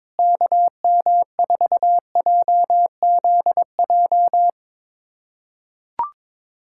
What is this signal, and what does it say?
Morse code, heard as a keyed tone, about 22 wpm, 700 Hz. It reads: KM4JZJ
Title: USA Callsign DE Callsign - 6 Characters - 22wpm